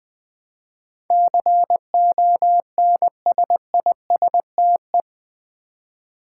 Morse code, heard as a keyed tone, about 20 wpm, 700 Hz. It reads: CONSISTE
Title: Build A Word - 4 Letters At A Time - 10-Letter Words - 20wpm